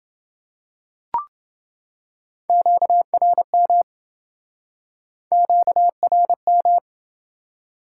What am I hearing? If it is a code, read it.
Morse code, heard as a keyed tone, about 27 wpm, 700 Hz. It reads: QRM QRM